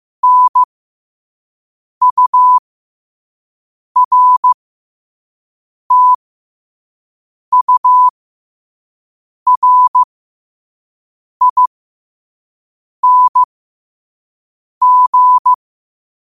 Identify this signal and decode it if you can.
Morse code, heard as a keyed tone, about 15 wpm, spaced as for 5 wpm, 1000 Hz. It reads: NURTURING
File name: qotdfiles / 5wpm